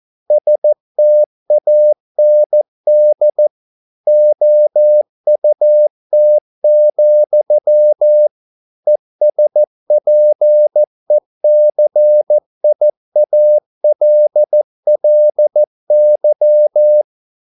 Morse code, heard as a keyed tone, about 14 wpm, 600 Hz. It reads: STAND OUT, ESPECIALLY